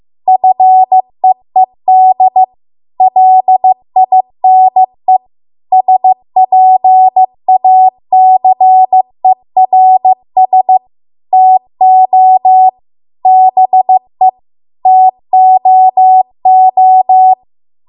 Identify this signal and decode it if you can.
Morse code, heard as a keyed tone, about 15 wpm, 750 Hz. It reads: FEED LINE SPACERS TO BE TOO